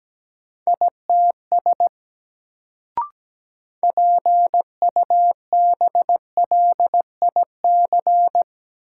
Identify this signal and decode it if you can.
Morse code, heard as a keyed tone, about 17 wpm, 700 Hz. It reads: ITS PUBLIC
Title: Sets of 2 Words - New Word Drill - 17wpm